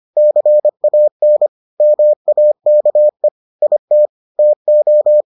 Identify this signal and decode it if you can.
Morse code, heard as a keyed tone, about 25 wpm, 600 Hz. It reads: CAN MAKE IT TO